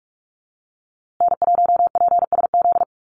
Morse code, heard as a keyed tone, about 44 wpm, 700 Hz. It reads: N1PSZ
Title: USA Call Signs Repeated 3x 45wpm